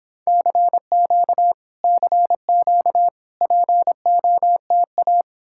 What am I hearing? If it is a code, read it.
Morse code, heard as a keyed tone, about 29 wpm, 700 Hz. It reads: CQ CQ POTA